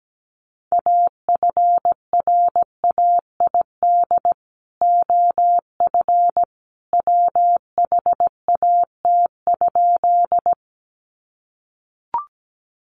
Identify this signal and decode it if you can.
Morse code, heard as a keyed tone, about 17 wpm, 700 Hz. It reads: AFRAID OF WHAT?
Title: Sentences from Top 1500 Words 17wpm